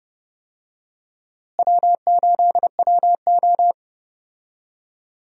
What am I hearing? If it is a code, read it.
Morse code, heard as a keyed tone, about 30 wpm, 700 Hz. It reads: W8WO
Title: USA Call Signs - 4 Characters - Repeated 2x - 30wpm